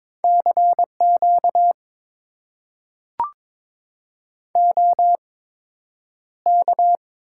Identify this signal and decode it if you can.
Morse code, heard as a keyed tone, about 22 wpm, spaced as for 9 wpm, 700 Hz. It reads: CQ OK
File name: Top 90 QSO Elements - 8x Character Spacing - 22wpm